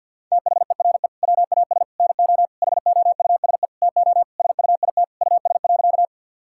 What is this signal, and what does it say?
Morse code, heard as a keyed tone, about 50 wpm, 700 Hz. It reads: THERE WAS NO HOUSE TO SUIT US.
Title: Sentences from Top 1600 Words 50wpm